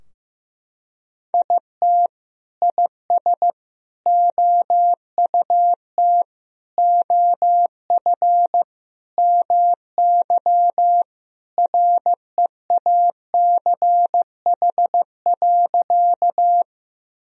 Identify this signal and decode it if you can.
Morse code, heard as a keyed tone, about 15 wpm, 700 Hz. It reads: IT IS OUT OF MY REACH.
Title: Sentences from Top 1000 Words - Review Part 2 - 15wpm